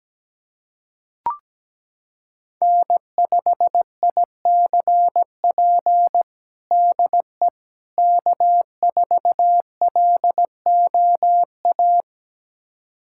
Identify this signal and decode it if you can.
Morse code, heard as a keyed tone, about 17 wpm, 700 Hz. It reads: N5ICP DE K4LOA
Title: USA Callsign DE Callsign - 5 Characters - 17wpm